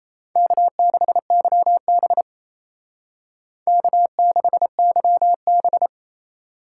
Morse code, heard as a keyed tone, about 35 wpm, 700 Hz. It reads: K6YB K6YB